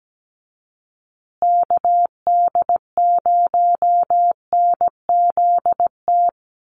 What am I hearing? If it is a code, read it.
Morse code, heard as a keyed tone, about 17 wpm, 700 Hz. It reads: KD0NZT
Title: USA Call Signs Repeated 3x 17wpm